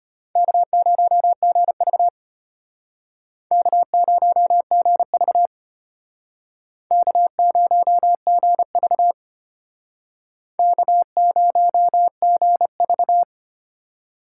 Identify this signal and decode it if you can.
Morse code, heard as a keyed tone, about 32 wpm, 700 Hz. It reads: K0GV K0GV K0GV K0GV